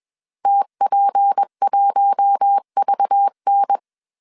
Morse code, heard as a keyed tone, about 21 wpm, 800 Hz. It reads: TP1VN
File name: CW / DET / 1 6 1500C S B 20wpm 2048